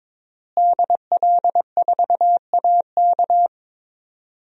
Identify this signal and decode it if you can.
Morse code, heard as a keyed tone, about 22 wpm, 700 Hz. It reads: DL4AK
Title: DXCC-100 Call Signs - 5 Characters - Repeated 2x - 22wpm